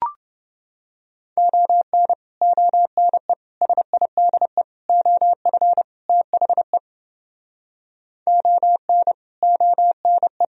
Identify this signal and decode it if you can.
Morse code, heard as a keyed tone, about 29 wpm, 700 Hz. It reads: ON ONE SIDE OF THE ON ONE